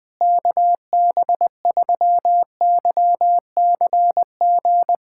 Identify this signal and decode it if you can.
Morse code, heard as a keyed tone, about 21 wpm, 700 Hz. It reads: KB3YCG
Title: USA Callsign DE Callsign - 6 Characters - 20wpm